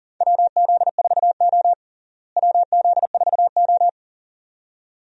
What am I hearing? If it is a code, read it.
Morse code, heard as a keyed tone, about 40 wpm, 700 Hz. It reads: WZ4O WZ4O
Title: USA Call Signs - 4 Characters - Repeated 2x - 40wpm